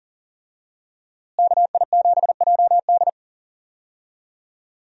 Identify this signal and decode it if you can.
Morse code, heard as a keyed tone, about 40 wpm, 700 Hz. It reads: KI7JD
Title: USA Call Signs - 5 Characters - Repeated 2x - 40wpm